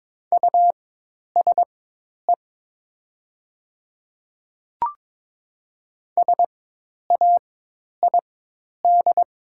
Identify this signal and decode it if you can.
Morse code, heard as a keyed tone, about 22 wpm, spaced as for 10 wpm, 700 Hz. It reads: USE SAID
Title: 4x Character Spacing - Top 100 Words - 22wpm